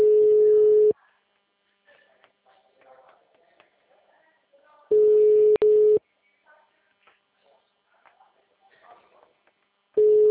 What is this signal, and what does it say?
Ringback tone (Europe, 425 Hz), with speech.
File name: out- -2010- - - .55598